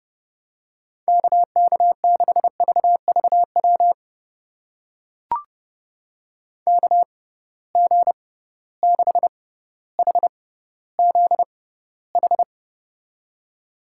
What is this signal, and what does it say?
Morse code, heard as a keyed tone, about 30 wpm, spaced as for 15 wpm, 700 Hz. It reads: KK6VVW KG6HZH
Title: USA Call Signs - 6 Characters - 6x Character Spacing - 30wpm